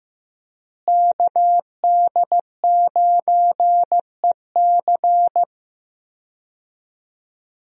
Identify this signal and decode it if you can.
Morse code, heard as a keyed tone, about 15 wpm, 700 Hz. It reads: KD9EC
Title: USA Call Signs - 5 Characters - Repeated 2x - 15wpm